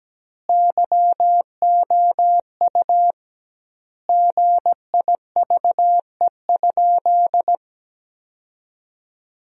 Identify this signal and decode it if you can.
Morse code, heard as a keyed tone, about 17 wpm, 700 Hz. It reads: YOU GIVE?
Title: 2x Word-Spacing - Sentences from Top 200 Words - 17wpm